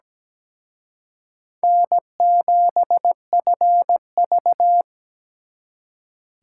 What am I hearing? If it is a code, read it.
Morse code, heard as a keyed tone, about 20 wpm, 700 Hz. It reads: N7FV
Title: Speed Racing - USA 1x2 Call Signs 17wpm